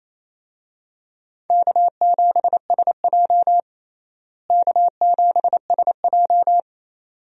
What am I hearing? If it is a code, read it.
Morse code, heard as a keyed tone, about 28 wpm, 700 Hz. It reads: K7SJ K7SJ
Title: USA Call Signs - 4 Characters - Repeated 2x - 28wpm